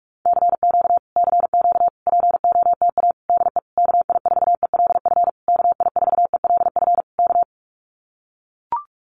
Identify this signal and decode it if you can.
Morse code, heard as a keyed tone, about 44 wpm, 700 Hz. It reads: CQ CQ POTA DE KI4ELF KI4ELF K
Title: Common QSO Phrases Encore 45wpm